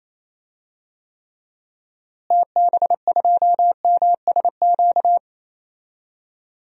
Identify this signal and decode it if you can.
Morse code, heard as a keyed tone, about 28 wpm, 700 Hz. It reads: TB2MSQ
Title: DXCC-100 Call Signs - 6 Characters - Repeated 3x - 28wpm